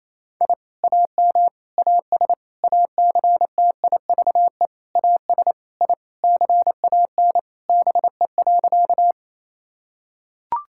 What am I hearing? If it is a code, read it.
Morse code, heard as a keyed tone, about 27 wpm, 700 Hz. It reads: I AM AS ACTIVE AS I CAN BE.
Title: Sentences from Top 1000 Words - Review Part 6 - 28wpm